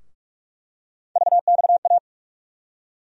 Morse code, heard as a keyed tone, about 46 wpm, 700 Hz. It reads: VXA
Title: USA Call Sign Postfixes - 3 Letters 45wpm